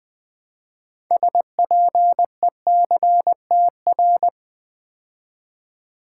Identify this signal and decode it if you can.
Morse code, heard as a keyed tone, about 20 wpm, 700 Hz. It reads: SPECTR
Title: Build A Word - 2 Letters At A Time - 8-Letter Words - 20wpm